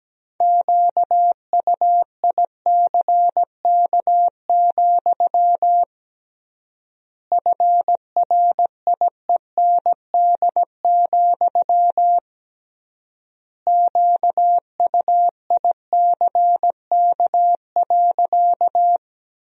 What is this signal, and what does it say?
Morse code, heard as a keyed tone, about 17 wpm, 700 Hz. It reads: QUICK, FRIEND, QUICK.
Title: 3x Word-Spacing - Sentences from Top 500 Words - 17wpm